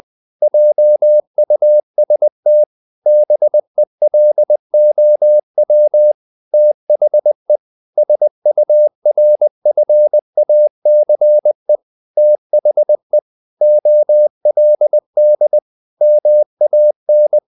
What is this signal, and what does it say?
Morse code, heard as a keyed tone, about 20 wpm, 600 Hz. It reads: JUST BELOW THE SURFACE THE OLD MAN
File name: the old man and the sea part 2 20wpm